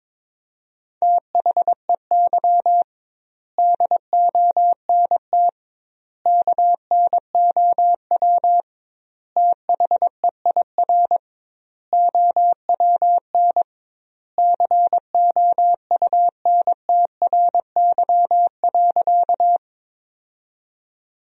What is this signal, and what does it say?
Morse code, heard as a keyed tone, about 22 wpm, 700 Hz. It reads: THEY DONT KNOW THEIR OWN COUNTRY.
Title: 2x Word-Spacing - Sentences from Top 300 Words - 22wpm